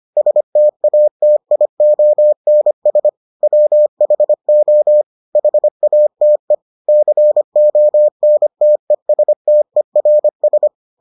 Morse code, heard as a keyed tone, about 25 wpm, 600 Hz. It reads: STATIONS WHO HATE CONTESTERS